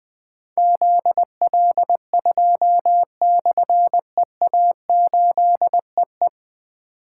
Morse code, heard as a keyed tone, about 20 wpm, 700 Hz. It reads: ZL2/EA8EE
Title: Difficult Call Signs 20wpm